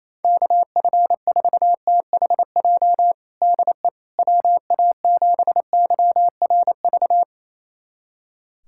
Morse code, heard as a keyed tone, about 28 wpm, 700 Hz. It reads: KF4THJ DE WA7YRV
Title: USA Callsign DE Callsign 28wpm